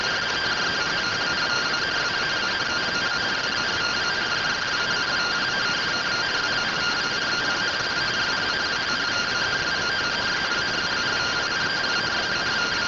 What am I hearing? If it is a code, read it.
Two-tone teleprinter signal, about 170 Hz shift.